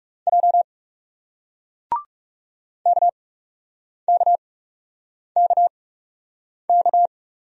Morse code, heard as a keyed tone, about 39 wpm, spaced as for 8 wpm, 700 Hz. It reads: J KKKK